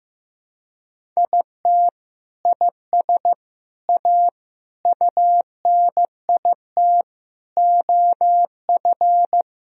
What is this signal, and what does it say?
Morse code, heard as a keyed tone, about 15 wpm, 700 Hz. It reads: IT IS A UNIT OF
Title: Sentences from Top 1000 Words - Review Part 2 - 15wpm